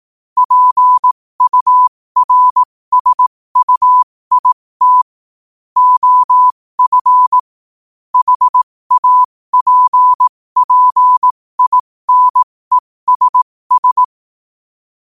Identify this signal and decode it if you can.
Morse code, heard as a keyed tone, about 18 wpm, spaced as for 15 wpm, 1000 Hz. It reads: PURSUIT OF HAPPINESS
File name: qotdfiles / 15wpm